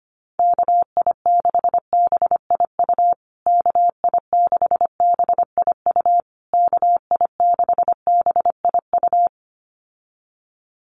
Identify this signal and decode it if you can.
Morse code, heard as a keyed tone, about 25 wpm, 700 Hz. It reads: KI6BIU KI6BIU KI6BIU
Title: USA Call Signs Repeated 3x 25wpm